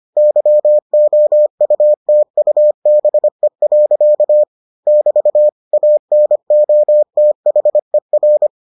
Morse code, heard as a keyed tone, about 25 wpm, 600 Hz. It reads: YOUTUBE. <BT> ANOTHER